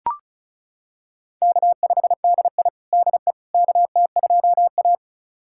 Morse code, heard as a keyed tone, about 36 wpm, 700 Hz. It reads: K5DI DE KT2A
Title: USA Callsign DE Callsign - 4 Characters - 35wpm